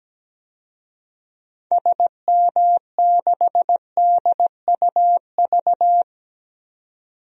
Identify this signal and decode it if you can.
Morse code, heard as a keyed tone, about 17 wpm, 700 Hz. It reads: SM6DUV
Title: DXCC-100 Call Signs - 6 Characters - 17wpm